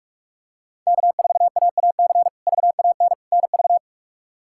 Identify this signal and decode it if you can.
Morse code, heard as a keyed tone, about 46 wpm, 700 Hz. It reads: K4AAC VAN NV
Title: K1USN SST QSO Practice - 45wpm